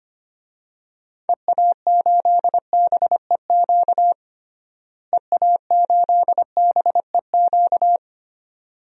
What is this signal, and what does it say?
Morse code, heard as a keyed tone, about 25 wpm, 700 Hz. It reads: EA8BEQ EA8BEQ
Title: DXCC-100 Call Signs - 6 Characters - Repeated 2x - 25wpm